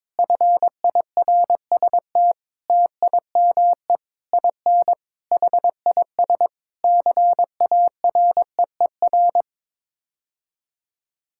Morse code, heard as a keyed tone, about 21 wpm, 700 Hz. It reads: FIRST TIME IN HIS CAREER
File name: practice / 22wpm / Sets of 7 Words - 22wpm